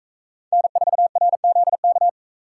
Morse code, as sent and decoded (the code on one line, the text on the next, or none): -. ....- .-. --.. -.-
N4RZK